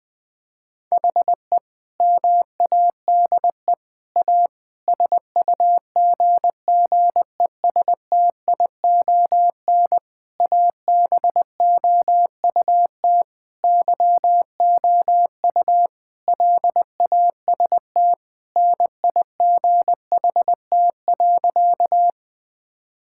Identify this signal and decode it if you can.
Morse code, heard as a keyed tone, about 20 wpm, 700 Hz. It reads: HE MADE A SUGGESTION ABOUT YOU LAST NIGHT.
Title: Sentences from Top 1600 Words 20wpm